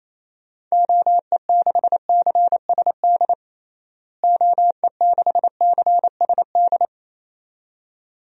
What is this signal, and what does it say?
Morse code, heard as a keyed tone, about 28 wpm, 700 Hz. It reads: OE6CSD OE6CSD
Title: DXCC-100 Call Signs - 6 Characters - Repeated 2x - 28wpm